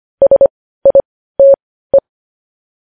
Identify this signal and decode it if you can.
Morse code, heard as a keyed tone, about 25 wpm, spaced as for 15 wpm, 550 Hz. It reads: SITE
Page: News Headlines in Morse Code at 15 WPM